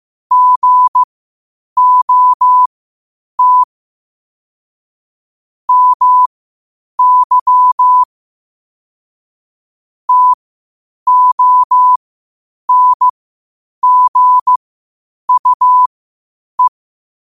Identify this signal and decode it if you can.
Morse code, heard as a keyed tone, about 15 wpm, spaced as for 8 wpm, 1000 Hz. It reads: GOT MY TONGUE
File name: qotdfiles / 8wpm